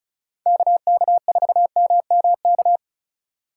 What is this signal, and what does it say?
Morse code, heard as a keyed tone, about 36 wpm, 700 Hz. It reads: KK4MMK
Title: USA Call Signs - 6 Characters - 2x Character Spacing - 35wpm